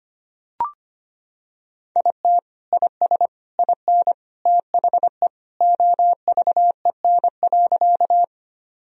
Morse code, heard as a keyed tone, about 25 wpm, 690 Hz. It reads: IT IS IN THE OVEN.
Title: Sentences from Top 1000 Words - Review Part 4 - 25wpm